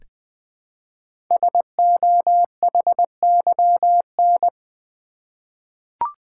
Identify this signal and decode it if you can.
Morse code, heard as a keyed tone, about 20 wpm, 700 Hz. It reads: SOHYN
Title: Code Groups - 5 Letters - 20wpm